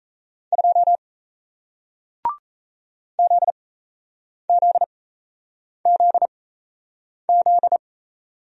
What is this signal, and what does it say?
Morse code, heard as a keyed tone, about 37 wpm, spaced as for 8 wpm, 700 Hz. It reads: 2 ZZZZ